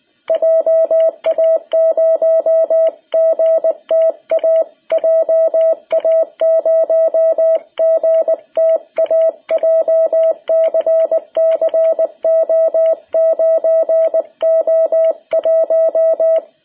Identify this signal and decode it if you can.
Morse code, heard as a keyed tone, about 19 wpm, 660 Hz. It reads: JA0GTA JA0GTA JCC O9O1
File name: jh8cbh / 2021cw